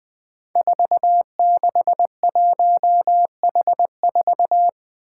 Morse code, heard as a keyed tone, about 20 wpm, 700 Hz. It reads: 461H4
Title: Five Challenging Letter-Number Combinations 20wpm